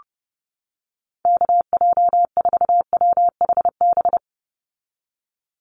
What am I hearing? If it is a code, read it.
Morse code, heard as a keyed tone, about 30 wpm, 700 Hz. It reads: KJ4WHB
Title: USA Call Signs 30wpm